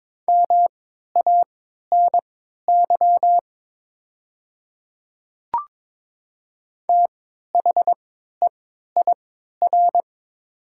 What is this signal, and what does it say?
Morse code, heard as a keyed tone, about 22 wpm, spaced as for 12 wpm, 700 Hz. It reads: MANY THEIR
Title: 3x Character Spacing - Top 100 Words - 22wpm